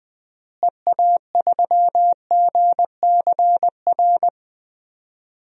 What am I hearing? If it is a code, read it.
Morse code, heard as a keyed tone, about 20 wpm, 700 Hz. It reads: EA3GCR
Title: DXCC-100 Call Signs - 6 Characters - Repeated 3x - 20wpm